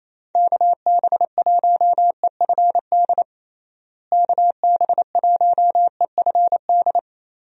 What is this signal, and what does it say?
Morse code, heard as a keyed tone, about 28 wpm, 700 Hz. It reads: KB1EFD KB1EFD
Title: USA Call Signs - 6 Characters - Repeated 2x - 28wpm